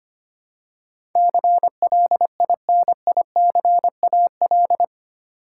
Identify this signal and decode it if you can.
Morse code, heard as a keyed tone, about 25 wpm, 700 Hz. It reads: CLINICAL
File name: Eight Letter Words - 25wpm